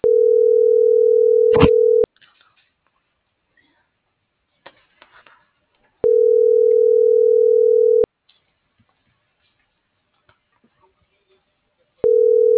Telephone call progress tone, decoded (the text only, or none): ringback tone